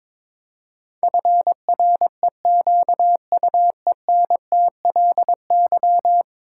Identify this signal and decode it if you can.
Morse code, heard as a keyed tone, about 22 wpm, 700 Hz. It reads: FREQUENTLY
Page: morse-code-ninja-practice-22wpm